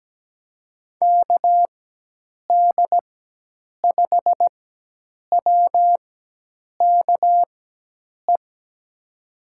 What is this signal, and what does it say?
Morse code, heard as a keyed tone, about 17 wpm, spaced as for 8 wpm, 700 Hz. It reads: KD5WKE
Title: USA Call Signs - 6 Characters - 4x Character Spacing - 17wpm